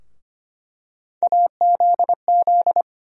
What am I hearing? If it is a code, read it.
Morse code, heard as a keyed tone, about 25 wpm, 700 Hz. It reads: AZZ
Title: USA Call Sign Postfixes - 3 Letters 25wpm